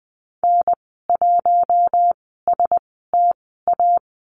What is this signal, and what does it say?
Morse code, heard as a keyed tone, about 20 wpm, spaced as for 14 wpm, 700 Hz. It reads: N1STA
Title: 042 Lesson 8g - Number 1 - Call Signs 20wpm